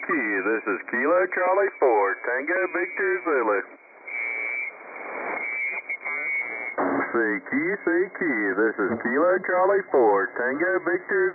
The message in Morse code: none